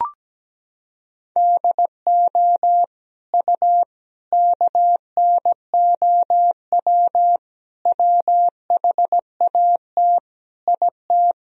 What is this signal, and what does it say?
Morse code, heard as a keyed tone, about 17 wpm, 700 Hz. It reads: DO U KNOW WHAT IT
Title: Sentences from Top 100 Words and Abbreviations 17wpm